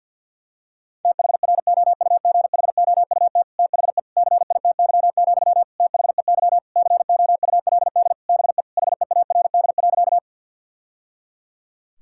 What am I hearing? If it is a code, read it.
Morse code, heard as a keyed tone, about 50 wpm, 700 Hz. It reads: THROUGHOUT THE CITY, THEY COULD BE HEARD.
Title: Sentences from Top 1000 Words - Review Part 5 - 50wpm